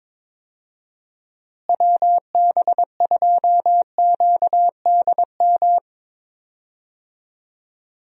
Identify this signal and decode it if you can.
Morse code, heard as a keyed tone, about 22 wpm, 700 Hz. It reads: WB2QDM
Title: USA Call Signs - 6 Characters - Repeated 3x - 22wpm